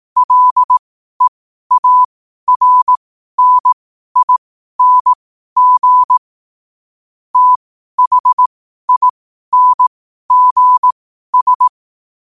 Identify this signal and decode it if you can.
Morse code, heard as a keyed tone, about 18 wpm, spaced as for 12 wpm, 1000 Hz. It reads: LEARNING THINGS
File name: qotdfiles / 12wpm